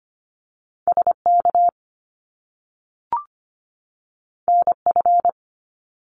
Morse code, heard as a keyed tone, about 25 wpm, 700 Hz. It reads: SK NF
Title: Top Two Letter Combinations in Words 25wpm